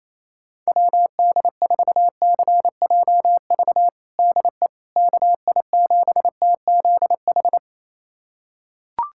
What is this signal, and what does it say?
Morse code, heard as a keyed tone, about 28 wpm, 700 Hz. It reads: WD4CJV DE KI7TZH
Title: USA Callsign DE Callsign - 6 Characters - 28wpm